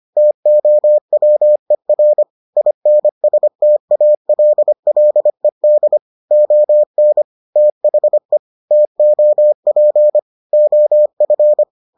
Morse code, heard as a keyed tone, about 25 wpm, 600 Hz. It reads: TOWER INSTALLED ON THE TOP OF